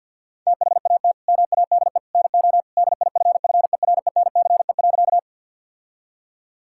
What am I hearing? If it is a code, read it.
Morse code, heard as a keyed tone, about 50 wpm, 700 Hz. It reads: THAT MADE NO DIFFERENCE.
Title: Sentences from Top 1000 Words - Review Part 5 - 50wpm